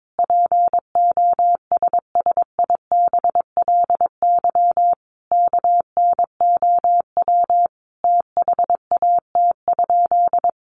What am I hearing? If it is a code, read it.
Morse code, heard as a keyed tone, about 22 wpm, 700 Hz. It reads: POSSIBLY KNOW THAT?
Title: Sentences from Top 1300 Words 22wpm